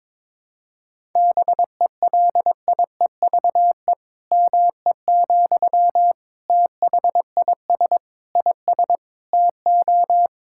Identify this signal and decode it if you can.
Morse code, heard as a keyed tone, about 22 wpm, 700 Hz. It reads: BELIEVE ME, THIS IS TO
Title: Sentences from Top 1000 Words - Review Part 6 - 22wpm